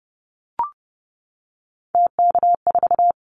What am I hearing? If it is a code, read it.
Morse code, heard as a keyed tone, about 30 wpm, 700 Hz. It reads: TK4
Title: Callsign Prefixes for Top 100 DXCC - 30wpm